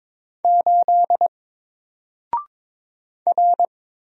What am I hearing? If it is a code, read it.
Morse code, heard as a keyed tone, about 22 wpm, 700 Hz. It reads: 8 R
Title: Single Character (Letter:Number:Punctuation:Prosign) Alt 22wpm